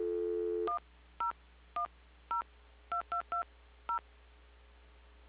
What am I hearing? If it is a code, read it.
DTMF, heard as telephone touch tones, tones of about 110 ms. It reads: 10102220
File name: recordings / PortlandUnion Millennium